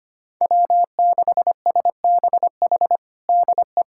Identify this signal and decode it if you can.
Morse code, heard as a keyed tone, about 25 wpm, 700 Hz. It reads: W6SBH DE
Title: USA Callsign DE Callsign 25wpm